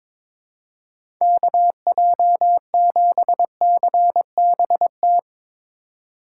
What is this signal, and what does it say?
Morse code, heard as a keyed tone, about 21 wpm, 700 Hz. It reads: KJ7CBT
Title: USA Call Signs - 6 Characters - 4x Character Spacing - 22wpm